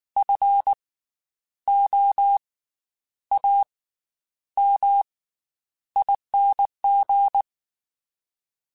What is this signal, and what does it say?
Morse code, heard as a keyed tone, about 19 wpm, 800 Hz. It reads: F O A M ING
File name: Special INGs / Special INGs-mp3s / Slow ing @18